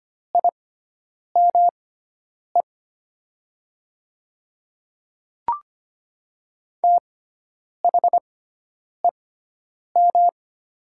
Morse code, heard as a keyed tone, about 25 wpm, spaced as for 9 wpm, 700 Hz. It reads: IME THEM
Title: 6x Character Spacing - Top 200 Words - 25wpm